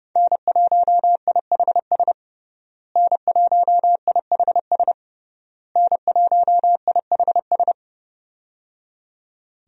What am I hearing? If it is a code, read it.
Morse code, heard as a keyed tone, about 30 wpm, 700 Hz. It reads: N1IHS N1IHS N1IHS
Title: USA Call Signs - 5 Characters - Repeated 3x - 30wpm